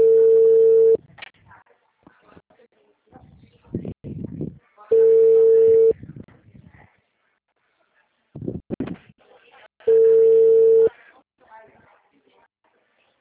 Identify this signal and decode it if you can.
Ringback tone (China, 450 Hz), with speech.